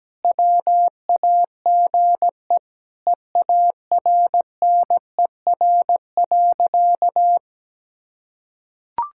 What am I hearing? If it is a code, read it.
Morse code, heard as a keyed tone, about 17 wpm, 700 Hz. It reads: WAGE EARNER.
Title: Pangram Challenge v2 - 17wpm